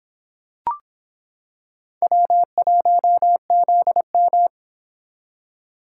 Morse code, heard as a keyed tone, about 25 wpm, 700 Hz. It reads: W1ZM